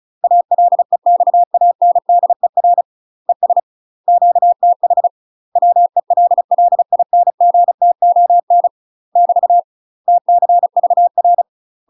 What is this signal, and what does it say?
Morse code, heard as a keyed tone, about 34 wpm, 700 Hz. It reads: ALEXANDER ES QTH WELLINGTON <BT> TCVR
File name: UK MASTER QSOs TRAINING / 35WPM / UK MASTER QSOs 0001